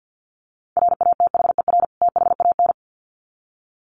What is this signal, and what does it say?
Morse code, heard as a keyed tone, about 50 wpm, 700 Hz. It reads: RATHER THAN
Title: Sets of 2 Words 50wpm